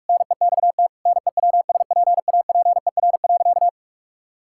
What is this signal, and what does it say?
Morse code, heard as a keyed tone, about 46 wpm, 700 Hz. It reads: NEXT NEWSPAPER.
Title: Sentences from Top 1000 Words - Review Part 6 - 45wpm